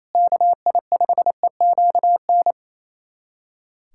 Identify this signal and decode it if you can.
Morse code, heard as a keyed tone, about 28 wpm, 700 Hz. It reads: KI5EQN
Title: USA Callsign DE Callsign - 6 Characters - 28wpm